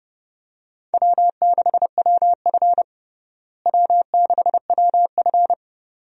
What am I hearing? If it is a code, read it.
Morse code, heard as a keyed tone, about 30 wpm, 700 Hz. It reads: W6WF W6WF
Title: USA Call Signs - 4 Characters - Repeated 2x - 30wpm